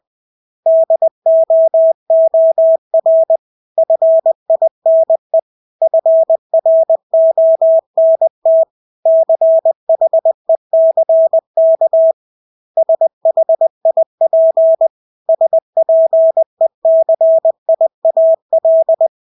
Morse code, heard as a keyed tone, about 20 wpm, 650 Hz. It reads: DOOR FINE FRONT CHECK SHIP SPECIAL
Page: Engelske ord 20 wpm